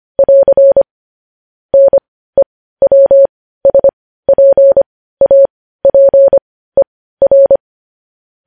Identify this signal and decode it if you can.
Morse code, heard as a keyed tone, about 25 wpm, spaced as for 15 wpm, 550 Hz. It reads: <AR> NEWSPAPER
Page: news-headlines-in-morse-code-at-15-wpm